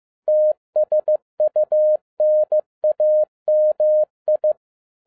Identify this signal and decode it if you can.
Morse code, heard as a keyed tone, about 15 wpm, 610 Hz. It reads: TSUNAMI
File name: 2011-10-mp3 / n02-FIELD DAY CONTEST QRV FROM SAKAE VILLAGE 15wpm